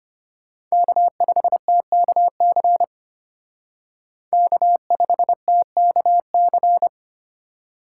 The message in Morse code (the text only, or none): K5TKC K5TKC